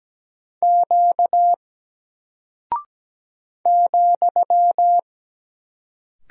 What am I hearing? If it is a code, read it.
Morse code, heard as a keyed tone, about 17 wpm, 700 Hz. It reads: Q ,
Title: Single Character (Letter:Number:Punctuation:Prosign) 17wpm